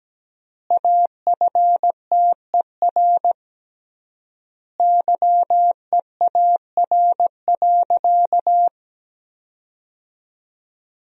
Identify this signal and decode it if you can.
Morse code, heard as a keyed tone, about 17 wpm, 700 Hz. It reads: AFTER YEAR.
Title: 3x Word-Spacing - Sentences from Top 300 Words - 17wpm